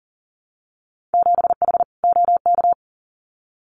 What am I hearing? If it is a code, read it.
Morse code, heard as a keyed tone, about 40 wpm, 700 Hz. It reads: 7H OK
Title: Winter Field Day Exchanges Encore 40wpm